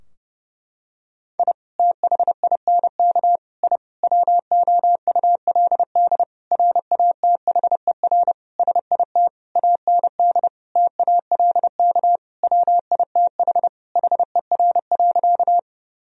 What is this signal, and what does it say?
Morse code, heard as a keyed tone, about 32 wpm, 700 Hz. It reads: I THINK I WOULD RATHER SIT AND TALK WITH HER.
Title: Sentences from Top 1000 Words - Review Part 4 - 30wpm